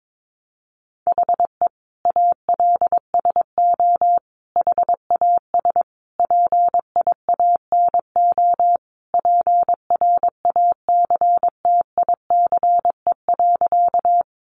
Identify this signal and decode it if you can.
Morse code, heard as a keyed tone, about 22 wpm, 700 Hz. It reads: HE ALSO HAS PIANO PRACTICE.
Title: Additional Easy Phrases 22wpm